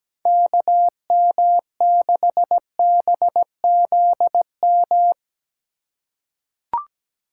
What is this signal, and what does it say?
Morse code, heard as a keyed tone, about 17 wpm, 700 Hz. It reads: KM6BZM
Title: USA Call Signs - 6 Characters - 2x Character Spacing - 17wpm